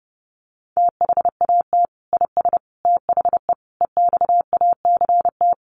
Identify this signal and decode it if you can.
Morse code, heard as a keyed tone, about 30 wpm, 700 Hz. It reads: THAT IS THE EXACT
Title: Sentences from Top 1300 Words 30wpm